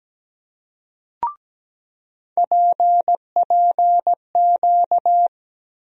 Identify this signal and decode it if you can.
Morse code, heard as a keyed tone, about 17 wpm, 700 Hz. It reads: PPQ
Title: USA Call Sign Postfixes - 3 Letters 17wpm